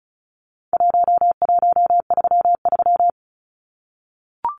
Morse code, heard as a keyed tone, about 34 wpm, 700 Hz. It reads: 1133
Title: Numbers - 4-Digit 35wpm